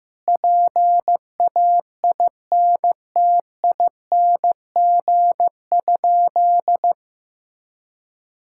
Morse code, heard as a keyed tone, about 15 wpm, 700 Hz. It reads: PAINTING?
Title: Sentences from Top 1000 Words - Review Part 6 - 15wpm